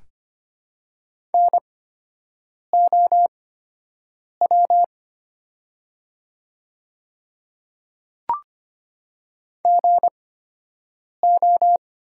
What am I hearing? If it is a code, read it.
Morse code, heard as a keyed tone, about 25 wpm, spaced as for 7 wpm, 700 Hz. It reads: NOW GO
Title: 8x Character Spacing - Top 100 Words - 25wpm